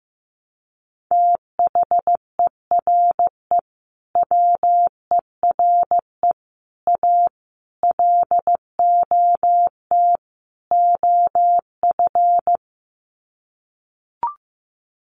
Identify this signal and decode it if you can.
Morse code, heard as a keyed tone, about 15 wpm, 700 Hz. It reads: THERE WERE A LOT OF
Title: Sets of 5 Words 15wpm